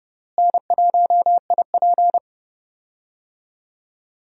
Morse code, heard as a keyed tone, about 30 wpm, 700 Hz. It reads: N1IP
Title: USA Call Signs - 4 Characters - Repeated 3x - 30wpm